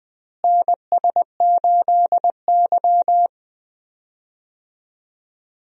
Morse code, heard as a keyed tone, about 20 wpm, 700 Hz. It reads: NS8Y
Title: USA Call Signs - 4 Characters - Repeated 2x - 20wpm